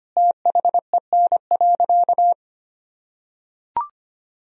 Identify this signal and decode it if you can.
Morse code, heard as a keyed tone, about 25 wpm, 700 Hz. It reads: THEN.
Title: Sentences from Top 200 Words and Abbreviations 25wpm